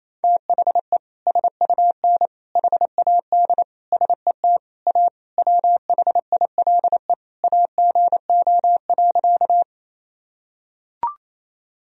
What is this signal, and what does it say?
Morse code, heard as a keyed tone, about 28 wpm, 700 Hz. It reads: THE SUN HAD SET A WHILE AGO.
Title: Sentences from Top 1000 Words - Review Part 1 - 28wpm